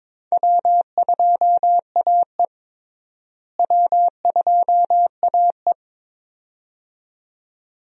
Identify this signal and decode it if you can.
Morse code, heard as a keyed tone, about 22 wpm, 700 Hz. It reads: W2AE W2AE
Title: USA Call Signs - 4 Characters - Repeated 2x - 22wpm